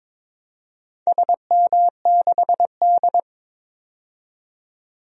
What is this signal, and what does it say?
Morse code, heard as a keyed tone, about 22 wpm, 700 Hz. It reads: SM6D
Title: DXCC-100 Call Signs - 4 Characters - 22wpm